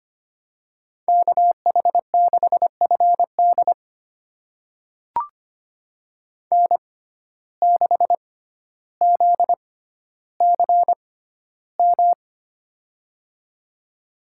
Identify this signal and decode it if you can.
Morse code, heard as a keyed tone, about 25 wpm, spaced as for 13 wpm, 700 Hz. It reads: KH6FD N6ZCM
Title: USA Call Signs - 5 Characters - 6x Character Spacing - 25wpm